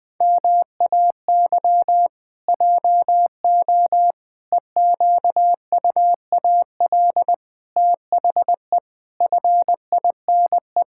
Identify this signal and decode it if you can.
Morse code, heard as a keyed tone, about 20 wpm, 700 Hz. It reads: MAY JO EQUAL THE FINE
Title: Pangram Challenge v2 - 20wpm